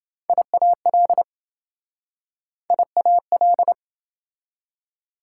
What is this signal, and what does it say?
Morse code, heard as a keyed tone, about 26 wpm, 700 Hz. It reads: IAL IAL